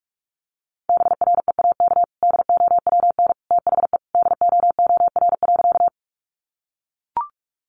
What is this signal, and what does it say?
Morse code, heard as a keyed tone, about 44 wpm, 700 Hz. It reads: BREAK DOWN THE DOOR.
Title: Sentences from Top 900 Words 45wpm